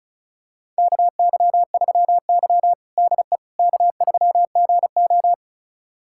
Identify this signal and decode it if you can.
Morse code, heard as a keyed tone, about 36 wpm, 700 Hz. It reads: KY3Y DE K3GO
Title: USA Callsign DE Callsign - 4 Characters - 35wpm